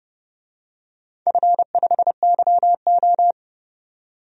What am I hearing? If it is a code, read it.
Morse code, heard as a keyed tone, about 30 wpm, 700 Hz. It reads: F5YO
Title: DXCC-100 Call Signs - 4 Characters - 30wpm